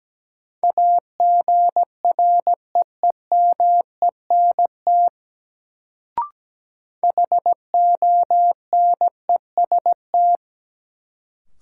Morse code, heard as a keyed tone, about 17 wpm, 700 Hz. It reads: AGREEMENT HONEST
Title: Sets of 5 Words Encore - New Word Drill - 17wpm